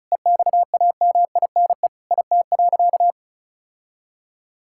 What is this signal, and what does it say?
Morse code, heard as a keyed tone, about 36 wpm, 700 Hz. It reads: EXAMINE IT.
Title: Sentences from Top 1600 Words 35wpm